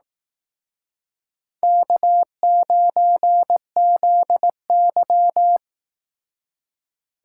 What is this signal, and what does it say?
Morse code, heard as a keyed tone, about 18 wpm, 700 Hz. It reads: K9ZY
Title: Speed Racing - USA 1x2 Call Signs 15wpm